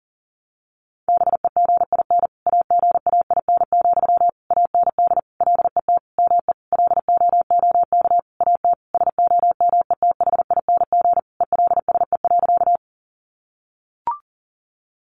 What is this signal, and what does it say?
Morse code, heard as a keyed tone, about 40 wpm, 700 Hz. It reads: BEGIN AGAIN, AND LET ME LOOK AT SOMETHING ELSE.
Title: Sentences from Top 800 Words 40wpm